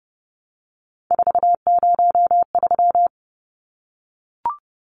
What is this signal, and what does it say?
Morse code, heard as a keyed tone, about 30 wpm, 700 Hz. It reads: 403
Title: Numbers - 3-Digit 30wpm